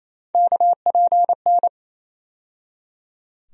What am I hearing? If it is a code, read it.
Morse code, heard as a keyed tone, about 28 wpm, 700 Hz. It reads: KPN
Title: USA Call Sign Postfixes - 3 Letters 28wpm